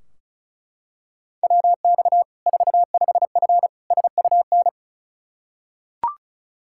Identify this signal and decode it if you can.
Morse code, heard as a keyed tone, about 36 wpm, 700 Hz. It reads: WX 45F SUN
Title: K1USN SST QSO Practice 35wpm